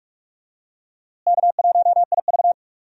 Morse code, heard as a keyed tone, about 42 wpm, 700 Hz. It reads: K1IV
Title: USA Call Signs - 4 Characters - Repeated 3x - 45wpm